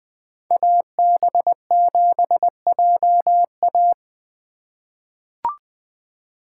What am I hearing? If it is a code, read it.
Morse code, heard as a keyed tone, about 20 wpm, 700 Hz. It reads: AB7JA
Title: USA Call Signs - 5 Characters - 2x Character Spacing - 20wpm